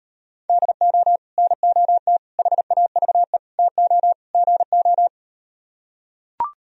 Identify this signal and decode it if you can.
Morse code, heard as a keyed tone, about 38 wpm, 700 Hz. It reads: DO NOT HAVE TO GO